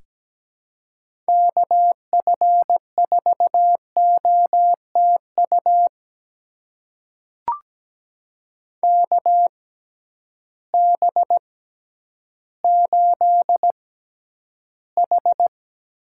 Morse code, heard as a keyed tone, about 17 wpm, spaced as for 10 wpm, 700 Hz. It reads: KF4OTU KB8H
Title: USA Call Signs - 6 Characters - 6x Character Spacing - 17wpm